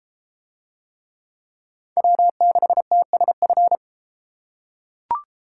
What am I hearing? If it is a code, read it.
Morse code, heard as a keyed tone, about 32 wpm, 700 Hz. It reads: W6TSF